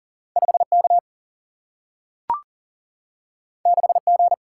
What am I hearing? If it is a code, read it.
Morse code, heard as a keyed tone, about 39 wpm, 700 Hz. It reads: 5 K 6 G
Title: Single-Letter Single-Number 40wpm